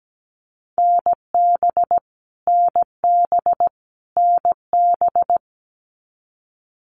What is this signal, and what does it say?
Morse code, heard as a keyed tone, about 17 wpm, 700 Hz. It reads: NB NB NB
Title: Canadian Provinces and Territories - Repeated 3x - 17wpm